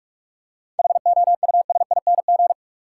Morse code, heard as a keyed tone, about 46 wpm, 700 Hz. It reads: HOUSING
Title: Seven Letter Words - 45wpm